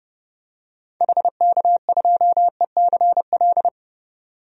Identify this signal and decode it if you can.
Morse code, heard as a keyed tone, about 32 wpm, 700 Hz. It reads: HK2ECL
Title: DXCC-100 Call Signs - 6 Characters - 30wpm